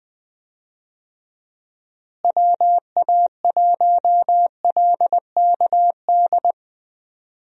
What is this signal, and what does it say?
Morse code, heard as a keyed tone, about 20 wpm, 700 Hz. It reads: WA1LKD
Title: USA Call Signs - 6 Characters - Repeated 3x - 20wpm